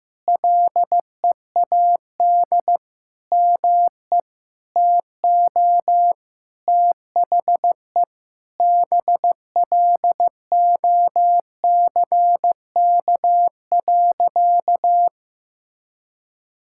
Morse code, heard as a keyed tone, about 15 wpm, 700 Hz. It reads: LEAD ME TO THE BLOCK.
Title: Sentences from Top 1600 Words 15wpm